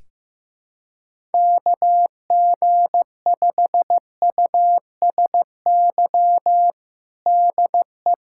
Morse code, heard as a keyed tone, about 15 wpm, 700 Hz. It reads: KG5USY DE
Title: USA Callsign DE Callsign - 6 Characters - 15wpm